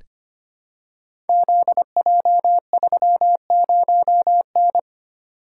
Morse code, heard as a keyed tone, about 25 wpm, 700 Hz. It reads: ZJ30N
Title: Code Groups - 5 Letter:Numbers - 25wpm